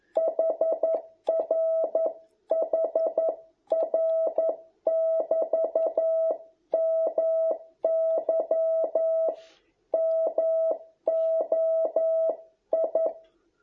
Morse code, heard as a keyed tone, about 11 wpm, 660 Hz. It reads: HR HR<BT> MY MOI